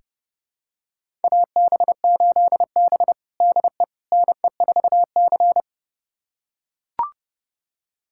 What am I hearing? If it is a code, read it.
Morse code, heard as a keyed tone, about 30 wpm, 700 Hz. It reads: AB8B DE NE4C
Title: USA Callsign DE Callsign - 4 Characters - 30wpm